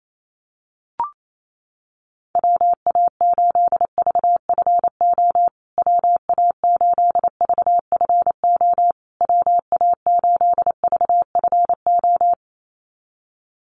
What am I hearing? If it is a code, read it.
Morse code, heard as a keyed tone, about 28 wpm, 700 Hz. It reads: WA8VFO WA8VFO WA8VFO
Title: USA Call Signs Repeated 3x 28wpm